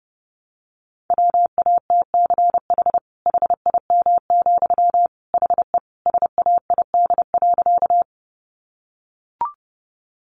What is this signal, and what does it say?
Morse code, heard as a keyed tone, about 30 wpm, 700 Hz. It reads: WATCH HIM, HE SAID.
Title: Sentences from Top 300 Words 30wpm